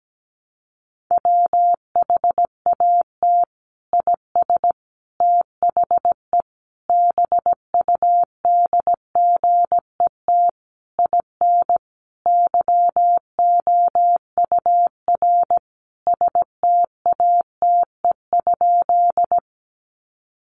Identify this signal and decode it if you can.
Morse code, heard as a keyed tone, about 17 wpm, 700 Hz. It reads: WHAT IS THE BUDGET IN YOUR STATE?
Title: Sentences from Top 1200 Words 17wpm